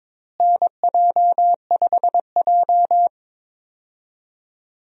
Morse code, heard as a keyed tone, about 22 wpm, 700 Hz. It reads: NJ5J
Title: USA Call Signs - 4 Characters - Repeated 3x - 22wpm